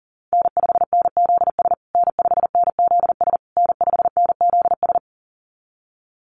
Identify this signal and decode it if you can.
Morse code, heard as a keyed tone, about 40 wpm, 700 Hz. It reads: N5NZS N5NZS N5NZS
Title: USA Call Signs Repeated 3x 40wpm